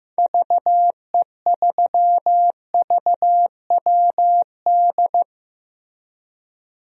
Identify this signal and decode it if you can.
Morse code, heard as a keyed tone, about 15 wpm, 700 Hz. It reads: VE3VWD
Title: DXCC-100 Call Signs - 6 Characters - Repeated 3x - 15wpm